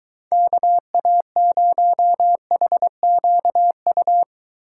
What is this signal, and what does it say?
Morse code, heard as a keyed tone, about 23 wpm, 700 Hz. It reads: KA0HQU